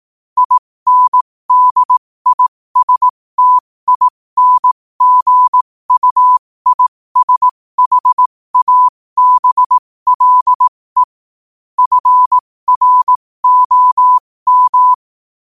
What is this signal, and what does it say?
Morse code, heard as a keyed tone, about 18 wpm, spaced as for 15 wpm, 1000 Hz. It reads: INDISTINGUISHABLE FROM